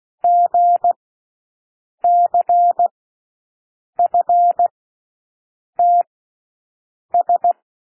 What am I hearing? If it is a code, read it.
Morse code, heard as a keyed tone, about 16 wpm, spaced as for 6 wpm, 700 Hz. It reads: GCFTS
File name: cwmp3 / L1 SOUND LETTERS 16 wpm